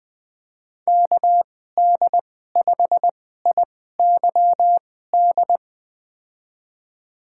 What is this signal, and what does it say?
Morse code, heard as a keyed tone, about 20 wpm, spaced as for 15 wpm, 700 Hz. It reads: KD5IYD
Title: USA Call Signs - 6 Characters - 2x Character Spacing - 20wpm